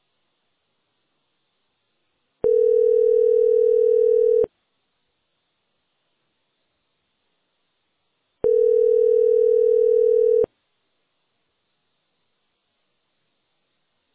Ringback tone (North America), 440 + 480 Hz.